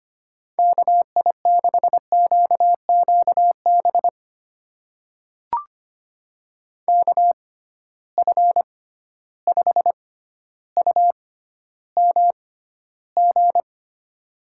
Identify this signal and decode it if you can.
Morse code, heard as a keyed tone, about 25 wpm, spaced as for 13 wpm, 700 Hz. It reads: KI6QQB KF5UMG
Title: USA Call Signs - 6 Characters - 6x Character Spacing - 25wpm